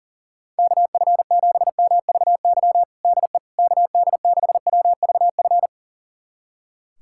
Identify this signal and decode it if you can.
Morse code, heard as a keyed tone, about 40 wpm, 700 Hz. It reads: KF7MVY DE KD6WVF
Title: USA Callsign DE Callsign - 6 Characters - 40wpm